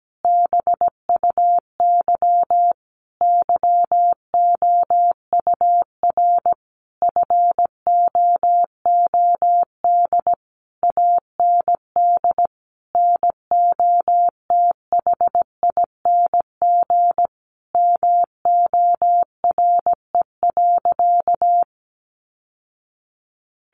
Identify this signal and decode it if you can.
Morse code, heard as a keyed tone, about 17 wpm, 700 Hz. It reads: BUY YOUR FOOD AND NOTHING MORE.
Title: Sentences from Top 800 Words 17wpm